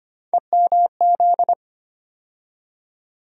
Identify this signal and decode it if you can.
Morse code, heard as a keyed tone, about 25 wpm, 700 Hz. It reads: EMZ
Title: USA Call Sign Postfixes - 3 Letters 25wpm